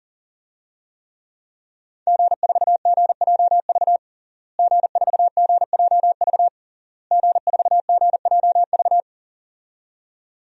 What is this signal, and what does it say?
Morse code, heard as a keyed tone, about 40 wpm, 700 Hz. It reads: G4GJV G4GJV G4GJV
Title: DXCC-100 Call Signs - 5 Characters - Repeated 3x - 40wpm